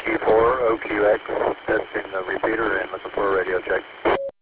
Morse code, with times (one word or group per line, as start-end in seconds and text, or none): none